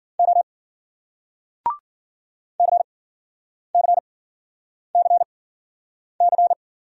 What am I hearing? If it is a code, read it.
Morse code, heard as a keyed tone, about 53 wpm, spaced as for 9 wpm, 700 Hz. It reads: O CCCC